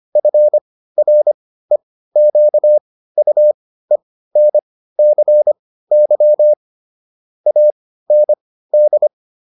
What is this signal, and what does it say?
Morse code, heard as a keyed tone, about 25 wpm, spaced as for 15 wpm, 600 Hz. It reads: FREQUENCY AND